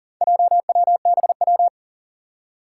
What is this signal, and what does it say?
Morse code, heard as a keyed tone, about 40 wpm, 700 Hz. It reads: JWBW
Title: Four Challenging Letter-Number Combinations 40wpm